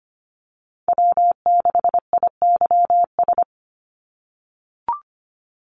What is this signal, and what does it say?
Morse code, heard as a keyed tone, about 25 wpm, 700 Hz. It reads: W6IYS
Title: NA SOTA Chaser Callsigns 25wpm